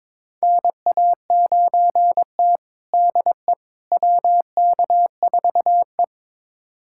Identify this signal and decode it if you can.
Morse code, heard as a keyed tone, about 22 wpm, 700 Hz. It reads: NA9T DE WK4E
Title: USA Callsign DE Callsign - 4 Characters - 22wpm